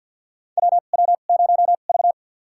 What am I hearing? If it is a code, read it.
Morse code, heard as a keyed tone, about 49 wpm, spaced as for 36 wpm, 700 Hz. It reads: WW0V
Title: USA Call Signs - 4 Characters - 2x Character Spacing - 50wpm